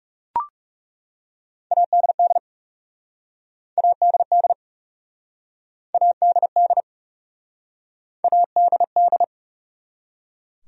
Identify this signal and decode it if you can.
Morse code, heard as a keyed tone, about 36 wpm, 700 Hz. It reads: ADD ADD ADD ADD
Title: Speed-Racing Top 200 Words 30wpm